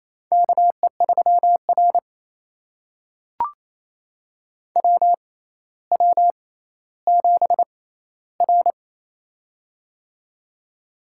Morse code, heard as a keyed tone, about 28 wpm, spaced as for 14 wpm, 700 Hz. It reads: KE3R WW7R
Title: USA Call Signs - 4 Characters - 6x Character Spacing - 28wpm